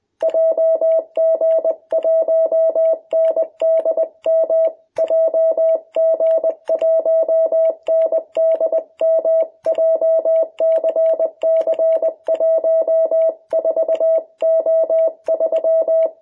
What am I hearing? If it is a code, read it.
Morse code, heard as a keyed tone, about 20 wpm, 660 Hz. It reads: JG1NDM JG1NDM JCC14O3
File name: jh8cbh / 2021cw